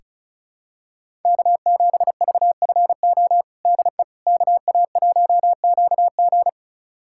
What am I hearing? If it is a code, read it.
Morse code, heard as a keyed tone, about 34 wpm, 700 Hz. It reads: K7VFO DE KA1QG
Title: USA Callsign DE Callsign - 5 Characters - 35wpm